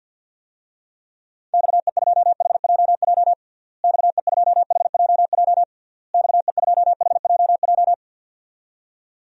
Morse code, heard as a keyed tone, about 51 wpm, 700 Hz. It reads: XE2HJJ XE2HJJ XE2HJJ
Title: DXCC-100 Call Signs - 6 Characters - Repeated 3x - 50wpm